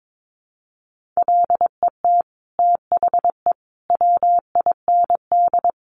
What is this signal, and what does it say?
Morse code, heard as a keyed tone, about 22 wpm, 700 Hz. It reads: LET THE WIND
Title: Sentences from Top 1500 Words 22wpm